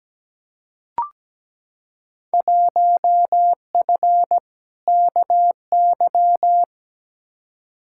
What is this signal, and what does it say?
Morse code, heard as a keyed tone, about 17 wpm, 700 Hz. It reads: 1F KY
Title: Field Day Exchanges Encore - 17wpm